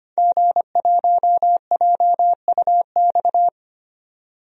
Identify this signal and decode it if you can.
Morse code, heard as a keyed tone, about 25 wpm, 700 Hz. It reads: G1JUX
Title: DXCC-100 Call Signs - 5 Characters - 25wpm